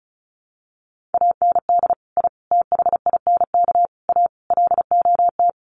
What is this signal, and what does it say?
Morse code, heard as a keyed tone, about 34 wpm, 700 Hz. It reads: AND I THINK A LOT
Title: Sets of 5 Words Encore2 35wpm